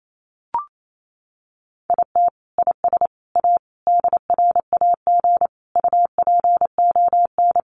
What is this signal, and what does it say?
Morse code, heard as a keyed tone, about 28 wpm, 700 Hz. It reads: IT IS A DRAG UPON
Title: Sentences from Top 1500 Words 28wpm